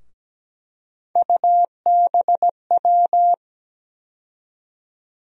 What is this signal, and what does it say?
Morse code, heard as a keyed tone, about 17 wpm, 700 Hz. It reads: UBW
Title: USA Call Sign Postfixes - 3 Letters 17wpm